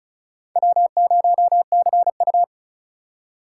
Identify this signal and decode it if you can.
Morse code, heard as a keyed tone, about 34 wpm, 700 Hz. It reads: W0CU
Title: USA Call Signs - 4 Characters - 4x Character Spacing - 35wpm